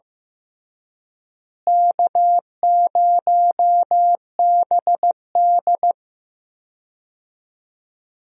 Morse code, heard as a keyed tone, about 16 wpm, 700 Hz. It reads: K0BD
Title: USA Call Signs - 4 Characters - Repeated 2x - 15wpm